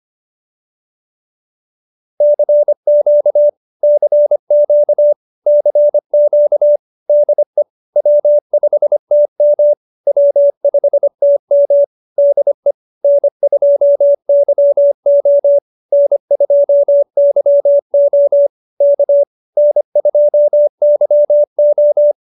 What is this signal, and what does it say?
Morse code, heard as a keyed tone, about 25 wpm, 590 Hz. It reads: CQ CQ CQ DE W5TM W5TM DE N2YO N2YO K N2YO